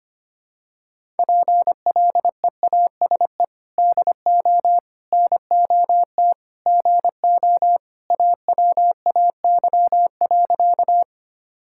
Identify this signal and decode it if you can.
Morse code, heard as a keyed tone, about 25 wpm, 700 Hz. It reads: PLEASE DO NOT GO AWAY.
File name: practice / 25wpm / Sentences with 4-Word Phrases - 25wpm